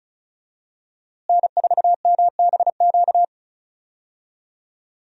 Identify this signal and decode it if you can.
Morse code, heard as a keyed tone, about 36 wpm, 700 Hz. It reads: N4MBQ
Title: USA Call Signs - 5 Characters - Repeated 3x - 35wpm